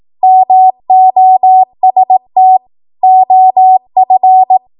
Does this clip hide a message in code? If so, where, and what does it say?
Yes: Morse code from 0.2 to 4.6 s, heard as a keyed tone, about 18 wpm, 750 Hz. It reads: MOST OF